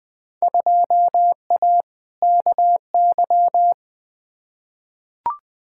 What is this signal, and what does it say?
Morse code, heard as a keyed tone, about 20 wpm, 700 Hz. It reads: 2A KY
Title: Field Day Exchanges - 20wpm